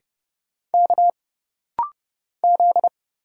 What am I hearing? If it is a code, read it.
Morse code, heard as a keyed tone, about 31 wpm, 700 Hz. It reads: K Z
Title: Single Letter-Number Alt - Rapid-Fire 30wpm